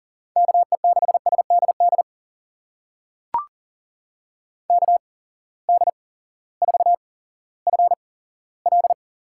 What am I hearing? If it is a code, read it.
Morse code, heard as a keyed tone, about 38 wpm, spaced as for 18 wpm, 700 Hz. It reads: KE6SDD KD4FL
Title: USA Call Signs - 6 Characters - 8x Character Spacing - 40wpm